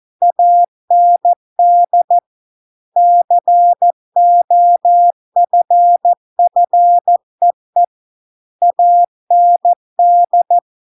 Morse code, heard as a keyed tone, about 14 wpm, 700 Hz. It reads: AND COFFEE AND